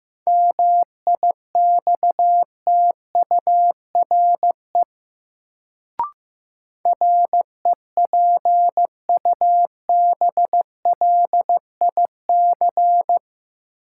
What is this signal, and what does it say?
Morse code, heard as a keyed tone, about 15 wpm, 700 Hz. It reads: MIXTURE REPUBLIC
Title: Sentences from Top 1500 Words - New Word Drill Alt - 15wpm